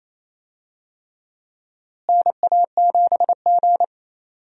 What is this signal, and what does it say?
Morse code, heard as a keyed tone, about 28 wpm, 700 Hz. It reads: NA7G
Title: USA Call Signs - 4 Characters - Repeated 3x - 28wpm